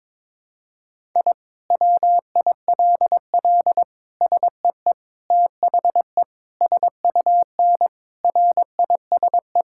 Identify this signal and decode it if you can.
Morse code, heard as a keyed tone, about 23 wpm, 700 Hz. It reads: I WILL SEE THE SUN RISE
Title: Sentences from Top 1000 Words - Review Part 6 - 22wpm